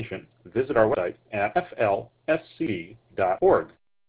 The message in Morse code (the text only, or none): none